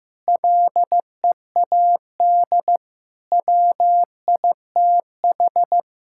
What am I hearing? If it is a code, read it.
Morse code, heard as a keyed tone, about 15 wpm, 700 Hz. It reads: LEAD WITH
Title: Sentences from Top 1000 Words - Review Part 5 - 15wpm